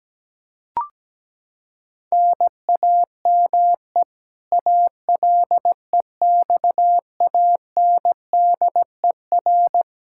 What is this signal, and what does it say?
Morse code, heard as a keyed tone, about 17 wpm, 700 Hz. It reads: NAME ALEXANDER
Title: Practice QSO - Exchange 1 - Beginning - 17wpm